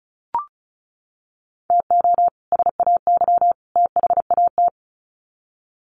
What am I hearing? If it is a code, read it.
Morse code, heard as a keyed tone, about 34 wpm, 700 Hz. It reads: TO SAY THAT
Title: Sets of 3 Words 35wpm